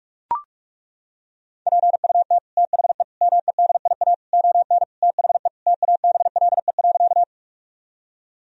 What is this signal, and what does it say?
Morse code, heard as a keyed tone, about 41 wpm, 700 Hz. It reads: PUT THE MEDIA ON THE TABLE.
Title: Sentences from Top 1000 Words - Review Part 4 - 45wpm